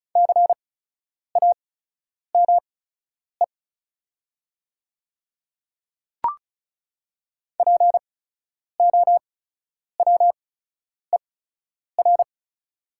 Morse code, heard as a keyed tone, about 36 wpm, spaced as for 10 wpm, 700 Hz. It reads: CAME POWER
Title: 8x Character Spacing - Top 500 Words - 35wpm